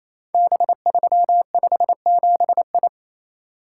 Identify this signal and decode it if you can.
Morse code, heard as a keyed tone, about 27 wpm, 700 Hz. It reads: B357I
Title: Code Groups - 5 Letter:Numbers - 28wpm